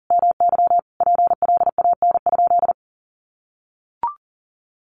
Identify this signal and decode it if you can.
Morse code, heard as a keyed tone, about 40 wpm, 700 Hz. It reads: MY PLAN?
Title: Sentences from Top 500 Words 40wpm